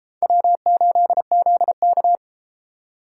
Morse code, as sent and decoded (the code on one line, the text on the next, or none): .-- ---.. --.. -.-
W8ZK